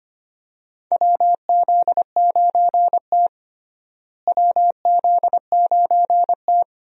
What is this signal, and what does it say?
Morse code, heard as a keyed tone, about 25 wpm, 700 Hz. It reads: WZ9T WZ9T
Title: USA Call Signs - 4 Characters - Repeated 2x - 25wpm